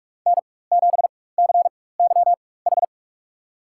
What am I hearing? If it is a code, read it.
Morse code, heard as a keyed tone, about 46 wpm, spaced as for 21 wpm, 700 Hz. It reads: N7CYH
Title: USA Call Signs - 5 Characters - 4x Character Spacing - 45wpm